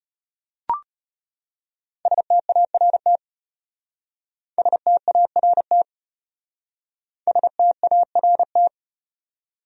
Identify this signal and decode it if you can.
Morse code, heard as a keyed tone, about 34 wpm, 700 Hz. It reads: START START START